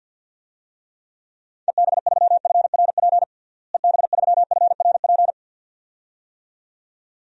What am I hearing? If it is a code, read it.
Morse code, heard as a keyed tone, about 53 wpm, 700 Hz. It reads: EB3FRP EB3FRP
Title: DXCC-100 Call Signs - 6 Characters - Repeated 2x - 50wpm